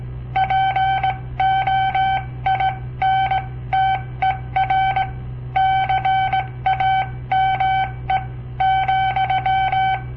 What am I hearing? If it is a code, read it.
Morse code, heard as a keyed tone, about 17 wpm, spaced as for 15 wpm, 760 Hz. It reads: POINTER CAME,
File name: papertapes / local / Code Practice-01